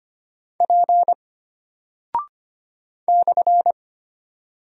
Morse code, heard as a keyed tone, about 25 wpm, 700 Hz. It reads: P /
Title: Single Character (Letter:Number:Punctuation:Prosign) Alt 25wpm